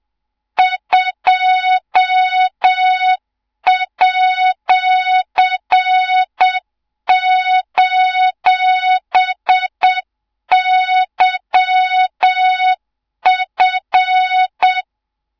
Morse code, heard as a keyed tone, about 7 wpm, 3700 Hz, with noise. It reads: 2@:YF
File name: Morze-HA2MN / hang / 24 GYAKORLAT